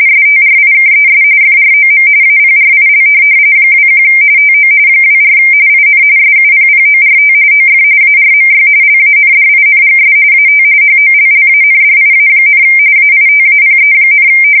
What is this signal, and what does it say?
RTTY, about 45.45 baud. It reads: STRIKES FROM APRIL 21 TO 24 IN SUPPORT OF ARMY LANDINGS AT HOLLANDIA, AITAPE,
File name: fsk170 / USS-New-Jersey